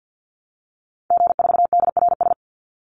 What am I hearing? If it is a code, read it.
Morse code, heard as a keyed tone, about 50 wpm, 700 Hz. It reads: G4DRS
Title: Call Signs - 50wpm